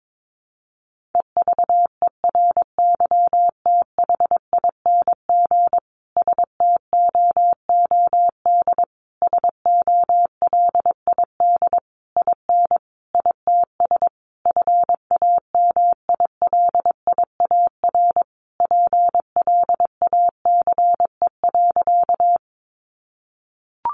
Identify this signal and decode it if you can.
Morse code, heard as a keyed tone, about 22 wpm, 700 Hz. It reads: EVERYTHING STOOD SOLID IN ITS FAMILIAR PLACE.
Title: Sentences from Top 1500 Words 22wpm